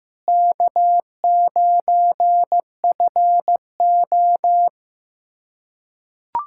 Morse code, heard as a keyed tone, about 15 wpm, 700 Hz. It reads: K9FO
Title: USA Callsign DE Callsign - 4 Characters - 15wpm